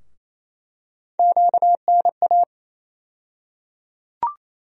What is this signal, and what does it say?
Morse code, heard as a keyed tone, about 28 wpm, 700 Hz. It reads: QNA
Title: USA Call Sign Postfixes - 3 Letters 28wpm